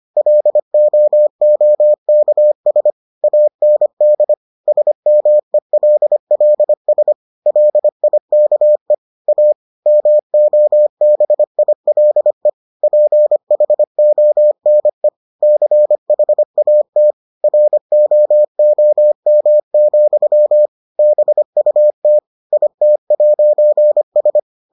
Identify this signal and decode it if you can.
Morse code, heard as a keyed tone, about 25 wpm, 600 Hz. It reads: LOOKS AND SMELLS LIKE A MOBILE PHONE CHAT ROOM, BUT IT'S